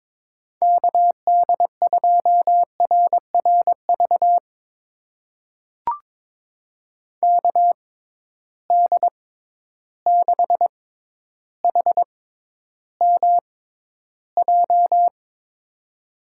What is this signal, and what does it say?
Morse code, heard as a keyed tone, about 21 wpm, spaced as for 11 wpm, 700 Hz. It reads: KD2RRV KD6HMJ
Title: USA Call Signs - 6 Characters - 6x Character Spacing - 22wpm